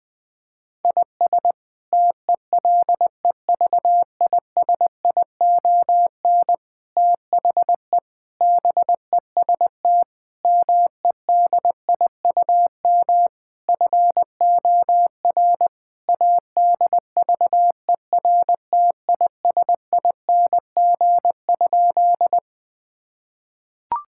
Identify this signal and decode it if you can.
Morse code, heard as a keyed tone, about 20 wpm, 700 Hz. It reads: IS TELEVISION THE BEST MEDIUM FOR ADVERTISING?
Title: Sentences from Top 1000 Words - Review Part 6 - 20wpm